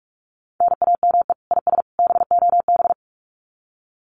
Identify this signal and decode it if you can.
Morse code, heard as a keyed tone, about 44 wpm, 700 Hz. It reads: NAME IS BOB
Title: Common QSO Phrases 45wpm